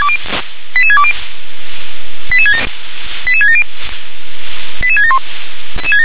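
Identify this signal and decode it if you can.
Five-tone selective call (ZVEI), heard as quick runs of five tones, tones of about 70 ms. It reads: 88620 88670 88671
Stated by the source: ZVEI five-tone call for address 88774, received by SDR.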